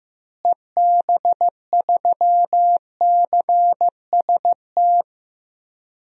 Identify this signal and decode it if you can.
Morse code, heard as a keyed tone, about 15 wpm, 700 Hz. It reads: EB3CST
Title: DXCC-100 Call Signs - 6 Characters - Repeated 2x - 15wpm